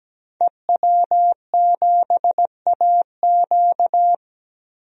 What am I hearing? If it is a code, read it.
Morse code, heard as a keyed tone, about 17 wpm, 700 Hz. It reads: EW7AQ
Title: DXCC-100 Call Signs - 5 Characters - 17wpm